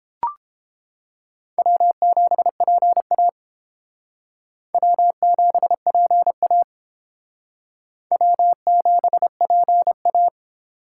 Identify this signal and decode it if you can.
Morse code, heard as a keyed tone, about 31 wpm, 700 Hz. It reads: W7PA W7PA W7PA